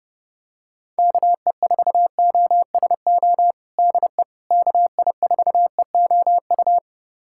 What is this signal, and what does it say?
Morse code, heard as a keyed tone, about 30 wpm, 700 Hz. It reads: KE4OSO DE KI4EOU
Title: USA Callsign DE Callsign - 6 Characters - 30wpm